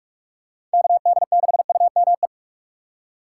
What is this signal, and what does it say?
Morse code, heard as a keyed tone, about 46 wpm, 700 Hz. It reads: KD6UME